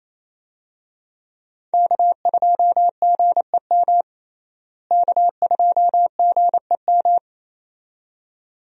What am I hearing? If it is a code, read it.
Morse code, heard as a keyed tone, about 28 wpm, 700 Hz. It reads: K2GEM K2GEM
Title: USA Call Signs - 5 Characters - Repeated 2x - 28wpm